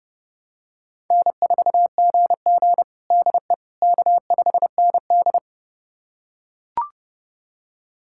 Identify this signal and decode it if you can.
Morse code, heard as a keyed tone, about 30 wpm, 700 Hz. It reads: N4GG DE K5ND
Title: USA Callsign DE Callsign - 4 Characters - 30wpm